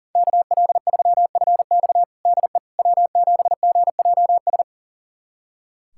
Morse code, heard as a keyed tone, about 40 wpm, 700 Hz. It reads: KL3FX DE W7GJS
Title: USA Callsign DE Callsign - 5 Characters - 40wpm